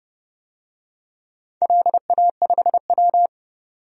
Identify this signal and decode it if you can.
Morse code, heard as a keyed tone, about 30 wpm, 700 Hz. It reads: LA5W
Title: DXCC-100 Call Signs - 4 Characters - 30wpm